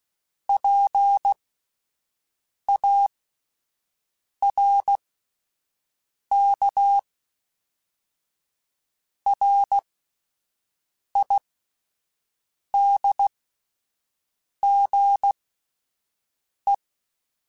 Morse code, heard as a keyed tone, about 16 wpm, spaced as for 5 wpm, 800 Hz. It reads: PARKRIDGE